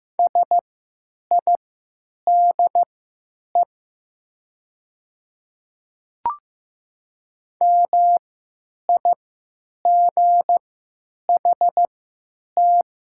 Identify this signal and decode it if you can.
Morse code, heard as a keyed tone, about 15 wpm, spaced as for 9 wpm, 700 Hz. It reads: SIDE MIGHT
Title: 3x Character Spacing - Top 300 Words - 15wpm